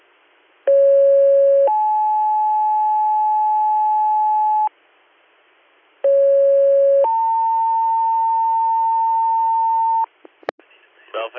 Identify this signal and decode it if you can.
Two-tone page, heard as a one-second tone, then a three-second tone, about 550 then 860 Hz, about 550 then 900 Hz, with speech. It is heard from 0.7 to 10.0 s.